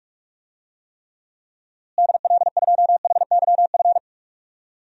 Morse code, heard as a keyed tone, about 44 wpm, 700 Hz. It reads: DL2HYF
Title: DXCC-100 Call Signs - 6 Characters - Repeated 2x - 45wpm